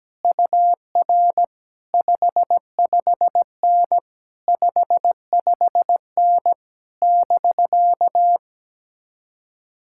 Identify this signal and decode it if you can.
Morse code, heard as a keyed tone, about 17 wpm, 700 Hz. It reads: UR 55N 55N <BK>
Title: Speed-Racing Common QSO Phrases - 17wpm